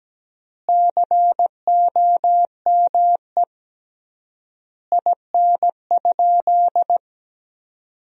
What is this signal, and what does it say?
Morse code, heard as a keyed tone, about 17 wpm, 700 Hz. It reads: COME IN?
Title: 3x Word-Spacing - Sentences from Top 100 Words - 17wpm